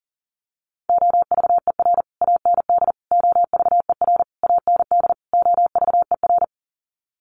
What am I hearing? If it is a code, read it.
Morse code, heard as a keyed tone, about 40 wpm, 700 Hz. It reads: OVER AND OVER AND OVER
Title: Sets of 5 Words Encore 40wpm